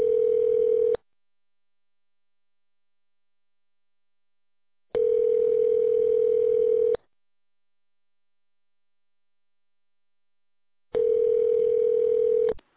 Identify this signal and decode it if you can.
Ringback tone (North America), 440 + 480 Hz.